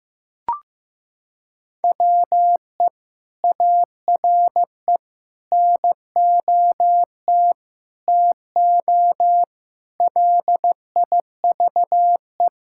Morse code, heard as a keyed tone, about 15 wpm, 700 Hz. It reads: WE ARE NOT TO LIVE
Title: Sentences from Top 1600 Words 15wpm